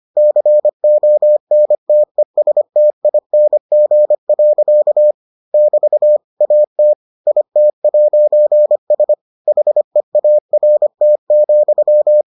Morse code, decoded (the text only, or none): CONTESTING. <BT> AT IT'S HEART,